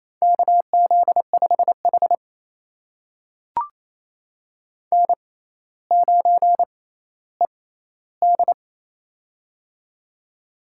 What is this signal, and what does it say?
Morse code, heard as a keyed tone, about 28 wpm, spaced as for 14 wpm, 700 Hz. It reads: KZ5H N9ED
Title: USA Call Signs - 4 Characters - 6x Character Spacing - 28wpm